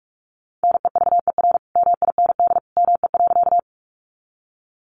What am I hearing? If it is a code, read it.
Morse code, heard as a keyed tone, about 46 wpm, 700 Hz. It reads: NEVER MIND ME.
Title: Sentences from Top 500 Words 45wpm